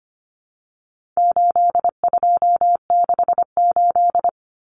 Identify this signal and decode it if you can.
Morse code, heard as a keyed tone, about 25 wpm, 700 Hz. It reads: 8268
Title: Numbers - 4-Digit 25wpm